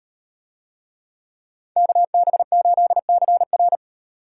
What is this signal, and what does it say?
Morse code, heard as a keyed tone, about 38 wpm, 700 Hz. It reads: KB8CR